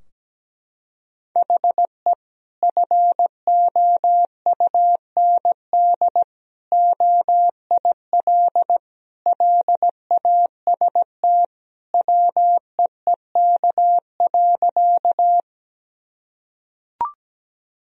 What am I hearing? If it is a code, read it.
Morse code, heard as a keyed tone, about 17 wpm, 700 Hz. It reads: HE FOUND OIL LAST WEEK.
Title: Sentences from Top 1000 Words - Review Part 4 - 17wpm